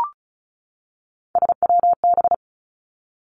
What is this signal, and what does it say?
Morse code, heard as a keyed tone, about 36 wpm, 700 Hz. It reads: SWB
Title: Three Challenging Letter-Number Combinations 35wpm